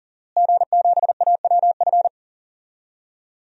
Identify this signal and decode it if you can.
Morse code, heard as a keyed tone, about 40 wpm, 700 Hz. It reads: G7AWF
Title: DXCC-100 Call Signs - 5 Characters - Repeated 2x - 40wpm